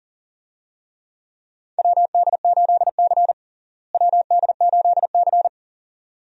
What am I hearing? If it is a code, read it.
Morse code, heard as a keyed tone, about 40 wpm, 700 Hz. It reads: WD8C WD8C
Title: USA Call Signs - 4 Characters - Repeated 2x - 40wpm